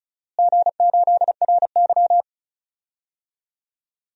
Morse code, heard as a keyed tone, about 34 wpm, 700 Hz. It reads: G8RY
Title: DXCC-100 Call Signs - 4 Characters - Repeated 2x - 35wpm